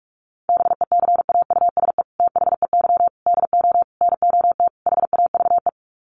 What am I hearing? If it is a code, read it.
Morse code, heard as a keyed tone, about 44 wpm, 690 Hz. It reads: BECAUSE THEY DO NOT HAVE
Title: Sets of 5 Words Encore2 45wpm